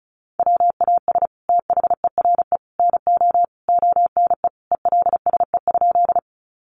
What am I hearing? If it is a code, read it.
Morse code, heard as a keyed tone, about 34 wpm, 700 Hz. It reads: WAS THERE NO ONE ELSE?
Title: Sentences from Top 800 Words 35wpm